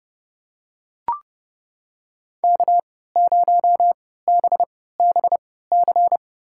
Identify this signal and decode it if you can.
Morse code, heard as a keyed tone, about 32 wpm, spaced as for 17 wpm, 700 Hz. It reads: K0BBC
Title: USA Call Signs - 3x Character Spacing - 30wpm